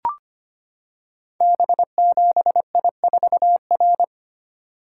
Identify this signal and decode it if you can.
Morse code, heard as a keyed tone, about 25 wpm, 700 Hz. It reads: B7I4R
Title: Code Groups - 5 Letter:Numbers - 25wpm